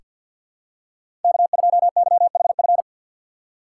Morse code, heard as a keyed tone, about 47 wpm, 700 Hz. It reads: K2YHF
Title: USA Call Signs - 3x Character Spacing - 50wpm